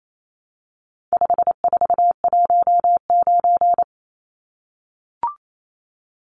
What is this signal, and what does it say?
Morse code, heard as a keyed tone, about 28 wpm, 700 Hz. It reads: 5419
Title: Numbers - 4-Digit 28wpm